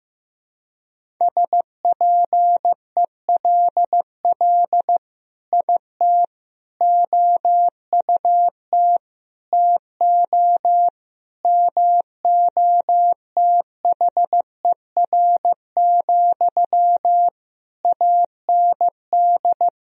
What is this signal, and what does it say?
Morse code, heard as a keyed tone, about 15 wpm, 700 Hz. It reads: SPELL IT OUT TO MOTHER, AND
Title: Sentences from Top 1000 Words - Review Part 1 - 15wpm